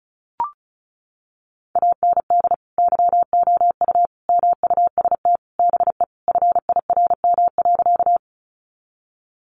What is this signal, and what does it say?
Morse code, heard as a keyed tone, about 34 wpm, 700 Hz. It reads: AND YOU MUST BE FIRM.
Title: Sentences from Top 1400 Words 35wpm